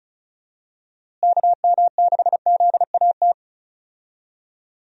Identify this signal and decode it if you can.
Morse code, heard as a keyed tone, about 36 wpm, 700 Hz. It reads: KM6ZAT
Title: USA Call Signs - 6 Characters - Repeated 2x - 35wpm